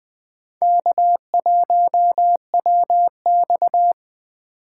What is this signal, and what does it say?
Morse code, heard as a keyed tone, about 20 wpm, 700 Hz. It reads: K1WX
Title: Speed Racing - USA 1x2 Call Signs 17wpm